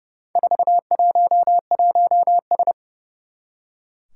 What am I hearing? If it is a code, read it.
Morse code, heard as a keyed tone, about 30 wpm, 700 Hz. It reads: 411S
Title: Four Challenging Letter-Number Combinations 30wpm